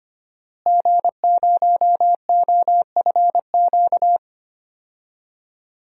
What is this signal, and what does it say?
Morse code, heard as a keyed tone, about 25 wpm, 700 Hz. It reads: G0OFQ
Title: DXCC-100 Call Signs - 5 Characters - Repeated 2x - 25wpm